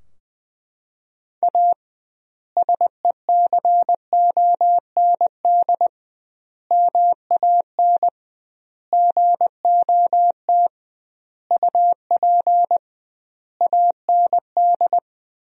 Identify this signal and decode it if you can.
Morse code, heard as a keyed tone, about 21 wpm, 700 Hz. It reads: A SECOND MAN GOT UP AND
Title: 2x Word-Spacing - Sentences from Top 300 Words - 20wpm